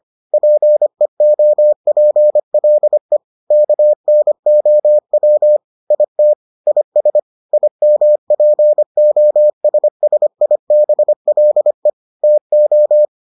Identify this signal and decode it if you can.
Morse code, heard as a keyed tone, about 25 wpm, 600 Hz. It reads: PEOPLE KNOW IT IS IMPOSSIBLE TO